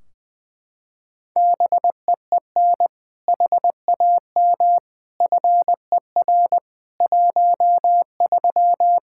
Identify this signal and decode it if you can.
Morse code, heard as a keyed tone, about 20 wpm, 700 Hz. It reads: BEEN HAM FER 13
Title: Practice QSO - Exchange 3 - More Info - 20wpm